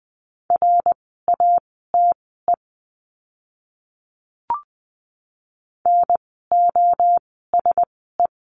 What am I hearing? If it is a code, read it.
Morse code, heard as a keyed tone, about 20 wpm, spaced as for 14 wpm, 700 Hz. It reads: RATE NOSE
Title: 054 Lesson 10f - Letter r - Words 20wpm